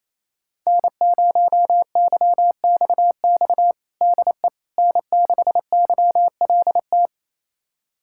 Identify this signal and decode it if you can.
Morse code, heard as a keyed tone, about 28 wpm, 700 Hz. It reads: N0YXX DE N6YLT
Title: USA Callsign DE Callsign - 5 Characters - 28wpm